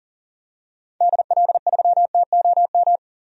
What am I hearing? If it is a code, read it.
Morse code, heard as a keyed tone, about 40 wpm, 700 Hz. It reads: DL3TOM
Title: DXCC-100 Call Signs - 6 Characters - 40wpm